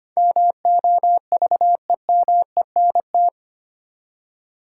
Morse code, heard as a keyed tone, about 25 wpm, 700 Hz. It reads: MOVEMENT
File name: Eight Letter Words - 25wpm